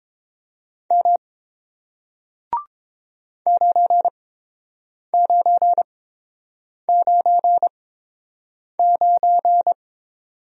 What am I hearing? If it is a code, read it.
Morse code, heard as a keyed tone, about 30 wpm, spaced as for 7 wpm, 700 Hz. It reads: M 9999